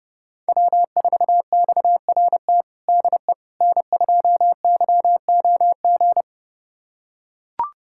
Morse code, heard as a keyed tone, about 30 wpm, 700 Hz. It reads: W4XRT DE N2YOG
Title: USA Callsign DE Callsign 30wpm